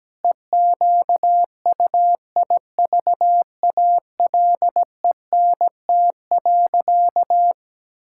Morse code, heard as a keyed tone, about 17 wpm, 700 Hz. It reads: EQUIVALENT.
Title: Sentences from Top 1600 Words 17wpm